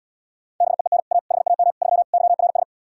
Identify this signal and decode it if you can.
Morse code, heard as a keyed tone, about 77 wpm, 700 Hz. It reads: BEEN A HAM 3 YRS